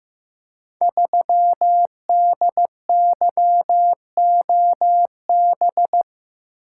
Morse code, heard as a keyed tone, about 15 wpm, 700 Hz. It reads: 3DYOB
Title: Code Groups - 5 Letter:Numbers - 15wpm